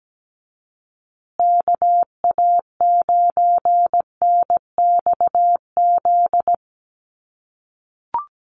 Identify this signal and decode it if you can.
Morse code, heard as a keyed tone, about 17 wpm, 700 Hz. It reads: KA9NXZ
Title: USA Call Signs Repeated 2x 17wpm